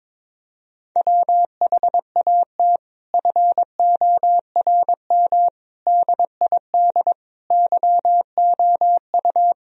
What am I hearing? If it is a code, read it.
Morse code, heard as a keyed tone, about 22 wpm, 700 Hz. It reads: WHAT FORM DID YOU
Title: Sentences from Top 1000 Words - Review Part 1 - 22wpm